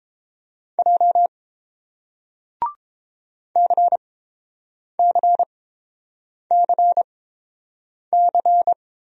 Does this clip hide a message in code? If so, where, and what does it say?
Yes: Morse code from 0.8 to 8.7 s, heard as a keyed tone, about 30 wpm, spaced as for 7 wpm, 700 Hz. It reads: J CCCC